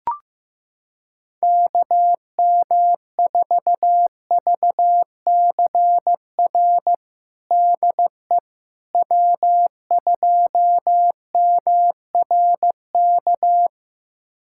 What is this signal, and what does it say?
Morse code, heard as a keyed tone, about 15 wpm, 700 Hz. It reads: KM4VCR DE W2MRK
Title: USA Callsign DE Callsign 15wpm